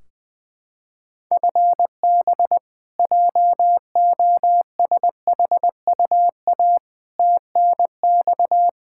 Morse code, heard as a keyed tone, about 20 wpm, 700 Hz. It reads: FB JOSHUA TNX
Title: K1USN SST QSO Practice 20wpm